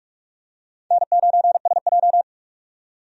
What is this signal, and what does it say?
Morse code, heard as a keyed tone, about 46 wpm, 700 Hz. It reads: N9SJ
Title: USA Call Signs - 4 Characters - 45wpm